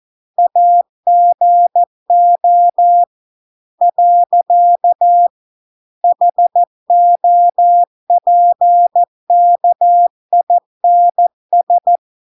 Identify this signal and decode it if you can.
Morse code, heard as a keyed tone, about 14 wpm, 700 Hz. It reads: AGO . HOPKINS